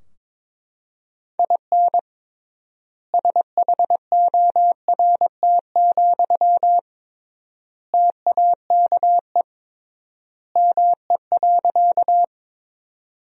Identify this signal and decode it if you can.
Morse code, heard as a keyed tone, about 22 wpm, 700 Hz. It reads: IN SHORT, TAKE ME.
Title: 3x Word-Spacing - Sentences from Top 400 Words - 22wpm